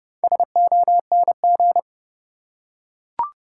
Morse code, heard as a keyed tone, about 30 wpm, 700 Hz. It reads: SONG
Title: Four Letter Words - 30wpm